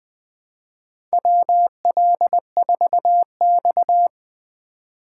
Morse code, heard as a keyed tone, about 20 wpm, 700 Hz. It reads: WL4X
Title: USA Call Signs - 4 Characters - 2x Character Spacing - 20wpm